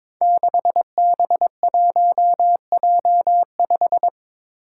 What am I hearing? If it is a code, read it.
Morse code, heard as a keyed tone, about 22 wpm, 700 Hz. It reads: 6B1J5
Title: Five Challenging Letter-Number Combinations 22wpm